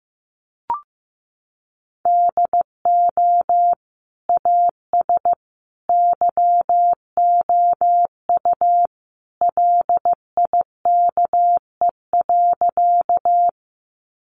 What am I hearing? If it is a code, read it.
Morse code, heard as a keyed tone, about 15 wpm, 700 Hz. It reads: DO AS YOU LIKE.
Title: Sentences from Top 100 Words 15wpm